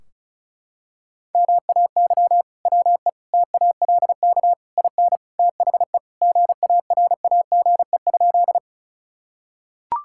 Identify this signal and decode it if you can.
Morse code, heard as a keyed tone, about 34 wpm, 700 Hz. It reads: MAY WE TALK IN THE GARAGE?
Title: Sentences from Top 1600 Words 35wpm